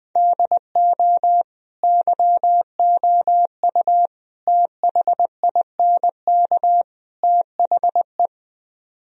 Morse code, heard as a keyed tone, about 20 wpm, 700 Hz. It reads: DO YOU THINK THE